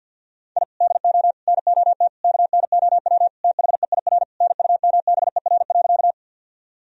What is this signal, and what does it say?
Morse code, heard as a keyed tone, about 52 wpm, 700 Hz. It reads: I DO NOT KNOW THEIR NUMBER.
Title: Sentences from Top 1000 Words - Review Part 1 - 50wpm